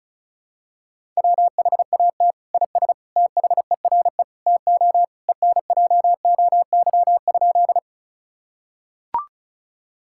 Morse code, heard as a keyed tone, about 34 wpm, 700 Hz. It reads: WHAT IS THERE TO ENJOY?
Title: Sentences from Top 1000 Words - Review Part 5 - 35wpm